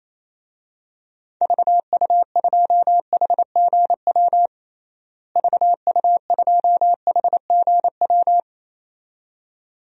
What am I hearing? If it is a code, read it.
Morse code, heard as a keyed tone, about 28 wpm, 700 Hz. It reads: VU2HGW VU2HGW
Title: DXCC-100 Call Signs - 6 Characters - Repeated 2x - 28wpm